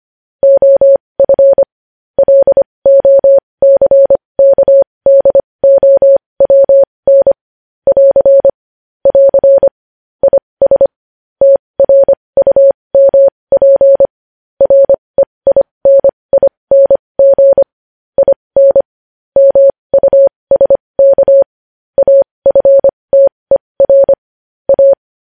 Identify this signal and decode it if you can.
Morse code, heard as a keyed tone, about 25 wpm, spaced as for 20 wpm, 550 Hz. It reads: OF LOCKDOWN <AR> <AR> IS TRUMP REINING IN MUSK AFTER A
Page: news-headlines-in-morse-code-at-20-wpm